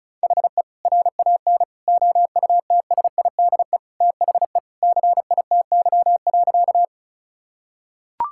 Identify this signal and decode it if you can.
Morse code, heard as a keyed tone, about 36 wpm, 700 Hz. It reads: HE RAN OUTSIDE THE CITY.
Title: Sentences from Top 1000 Words - Review Part 6 - 35wpm